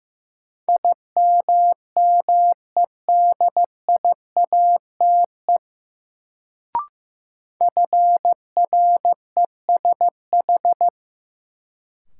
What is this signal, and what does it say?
Morse code, heard as a keyed tone, about 15 wpm, 700 Hz. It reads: IMMEDIATE FRESH
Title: Sentences from Top 1200 Words - New Word Drill Alt - 15wpm